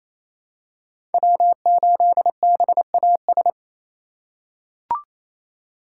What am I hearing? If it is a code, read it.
Morse code, heard as a keyed tone, about 28 wpm, 700 Hz. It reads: W8BAS
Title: USA Call Signs - 5 Characters - 4x Character Spacing - 28wpm